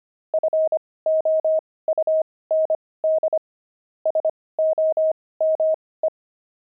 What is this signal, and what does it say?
Morse code, heard as a keyed tone, about 25 wpm, spaced as for 18 wpm, 630 Hz. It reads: FOUND SOME
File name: pangram-fs-18-cs-25-deep